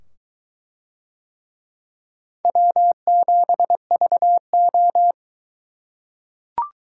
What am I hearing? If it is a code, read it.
Morse code, heard as a keyed tone, about 22 wpm, 700 Hz. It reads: W7VO